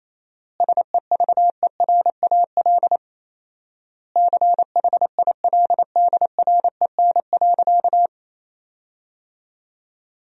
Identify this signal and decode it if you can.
Morse code, heard as a keyed tone, about 28 wpm, 700 Hz. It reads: SEVERAL CHILDREN.
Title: 4x Word-Spacing - Sentences from Top 400 Words - 28wpm